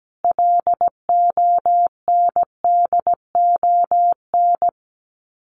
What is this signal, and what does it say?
Morse code, heard as a keyed tone, about 17 wpm, 700 Hz. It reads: LONDON
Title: Common QSO Phrases 17wpm